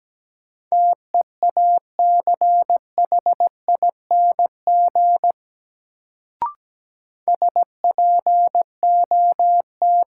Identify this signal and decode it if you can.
Morse code, heard as a keyed tone, about 17 wpm, 700 Hz. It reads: TEACHING SPOT
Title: Sentences from Top 1100 Words - New Word Drill - 17wpm